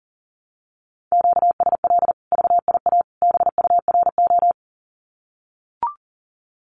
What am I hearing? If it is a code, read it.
Morse code, heard as a keyed tone, about 40 wpm, 700 Hz. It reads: QSL VIA BURO
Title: Common QSO Phrases 40wpm